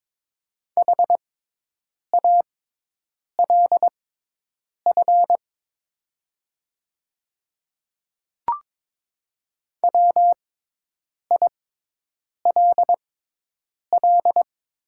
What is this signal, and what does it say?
Morse code, heard as a keyed tone, about 23 wpm, spaced as for 8 wpm, 690 Hz. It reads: HALF WILL
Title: 6x Character Spacing - Top 500 Words - 22wpm